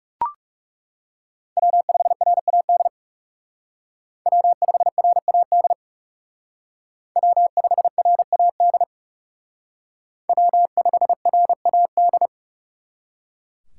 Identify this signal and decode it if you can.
Morse code, heard as a keyed tone, about 36 wpm, 700 Hz. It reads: W5RAD W5RAD W5RAD W5RAD
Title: Speed Racing - USA 1x3 Call Signs 30wpm